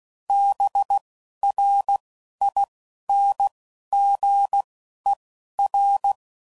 Morse code, heard as a keyed tone, about 16 wpm, spaced as for 12 wpm, 800 Hz. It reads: BRINGER